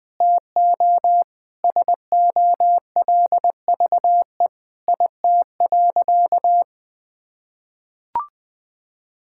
Morse code, heard as a keyed tone, about 20 wpm, 700 Hz. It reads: TO SOLVE IT.
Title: Sentences with 3-Word Phrases - 20wpm